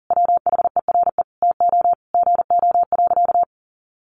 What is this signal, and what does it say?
Morse code, heard as a keyed tone, about 40 wpm, 700 Hz. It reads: WHERE TO GO.
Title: Sentences from Top 700 Words 40wpm